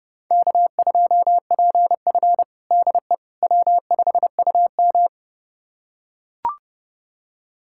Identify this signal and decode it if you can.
Morse code, heard as a keyed tone, about 30 wpm, 700 Hz. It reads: K2PF DE W5UM
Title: USA Callsign DE Callsign - 4 Characters - 30wpm